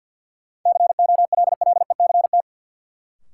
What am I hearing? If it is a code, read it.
Morse code, heard as a keyed tone, about 47 wpm, 700 Hz. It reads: COLLECT
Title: Sentences from Top 1600 Words - New Word Drill - 50wpm